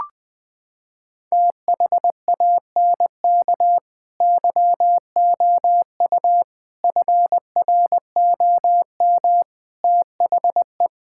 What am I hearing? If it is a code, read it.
Morse code, heard as a keyed tone, about 20 wpm, 700 Hz. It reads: THANK YOU FROM THE
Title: Sets of 8 Words - 20wpm